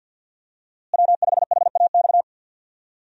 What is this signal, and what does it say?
Morse code, heard as a keyed tone, about 52 wpm, 700 Hz. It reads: W5HAX
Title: USA Call Signs - 5 Characters - Repeated 3x - 50wpm